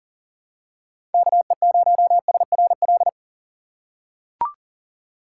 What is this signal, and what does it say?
Morse code, heard as a keyed tone, about 40 wpm, 700 Hz. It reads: KE0SRL
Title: K1USN SST QSO Practice - 40wpm